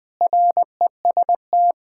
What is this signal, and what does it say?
Morse code, heard as a keyed tone, about 21 wpm, 700 Hz. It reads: REST
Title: Sentences with 5-Word Phrases - 20wpm